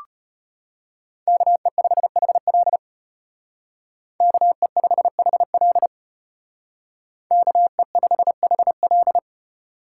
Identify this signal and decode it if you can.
Morse code, heard as a keyed tone, about 31 wpm, 700 Hz. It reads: KE5HL KE5HL KE5HL
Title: Speed Racing - USA 2x2 Call Signs 25wpm